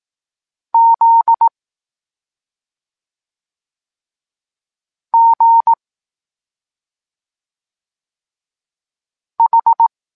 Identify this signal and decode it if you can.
Morse code, heard as a keyed tone, about 18 wpm, 920 Hz. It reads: Z G H
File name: Tekst-26 1